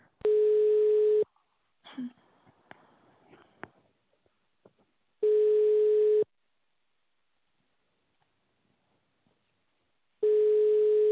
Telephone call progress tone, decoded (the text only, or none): ringback tone